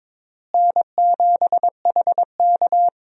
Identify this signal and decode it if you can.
Morse code, heard as a keyed tone, about 22 wpm, 700 Hz. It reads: N7HK
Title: USA Call Signs - 4 Characters - Repeated 2x - 22wpm